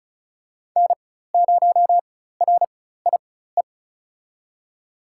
Morse code, heard as a keyed tone, about 36 wpm, spaced as for 16 wpm, 700 Hz. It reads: N0RIE
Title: USA Call Signs - 4x Character Spacing - 35wpm